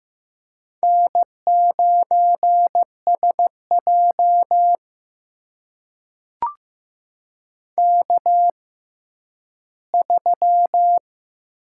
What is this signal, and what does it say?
Morse code, heard as a keyed tone, about 15 wpm, spaced as for 10 wpm, 700 Hz. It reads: N9SJ K3
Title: USA Call Signs - 4 Characters - 6x Character Spacing - 15wpm